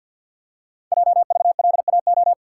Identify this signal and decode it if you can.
Morse code, heard as a keyed tone, about 52 wpm, 700 Hz. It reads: JVLAO
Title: Code Groups - 5 Letter:Numbers - 50wpm